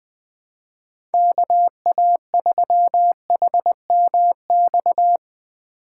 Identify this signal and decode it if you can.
Morse code, heard as a keyed tone, about 20 wpm, 700 Hz. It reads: KA3HMX
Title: USA Call Signs - 4x Character Spacing - 20wpm